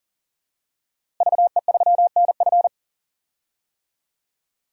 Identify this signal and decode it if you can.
Morse code, heard as a keyed tone, about 40 wpm, 700 Hz. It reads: VE3NF
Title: DXCC-100 Call Signs - 5 Characters - Repeated 2x - 40wpm